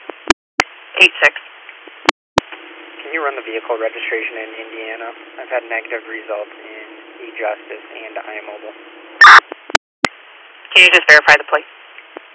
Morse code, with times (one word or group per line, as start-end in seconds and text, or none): none